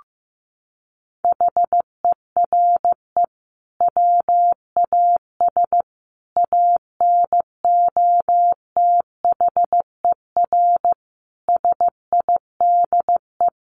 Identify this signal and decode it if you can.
Morse code, heard as a keyed tone, about 15 wpm, 700 Hz. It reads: HERE WAS ANOTHER SIDE
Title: Sentences from Top 1500 Words 15wpm